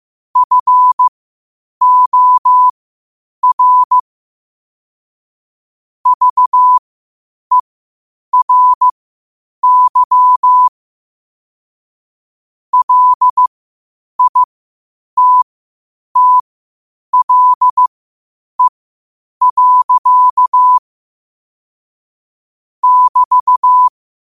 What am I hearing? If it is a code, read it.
Morse code, heard as a keyed tone, about 15 wpm, spaced as for 8 wpm, 1000 Hz. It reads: FOR VERY LITTLE. <BT>
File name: qotdfiles / 8wpm